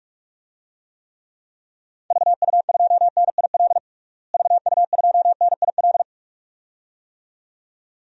Morse code, heard as a keyed tone, about 41 wpm, 700 Hz. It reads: VU2NIL VU2NIL
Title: DXCC-100 Call Signs - 6 Characters - Repeated 2x - 45wpm